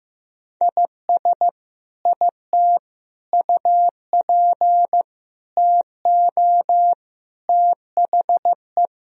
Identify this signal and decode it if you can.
Morse code, heard as a keyed tone, about 15 wpm, 700 Hz. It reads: IS IT UP TO THE
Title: Sentences from Top 1000 Words - Review Part 4 - 15wpm